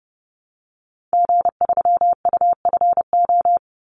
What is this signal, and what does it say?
Morse code, heard as a keyed tone, about 30 wpm, 700 Hz. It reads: G3UFO
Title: Call Signs - 30wpm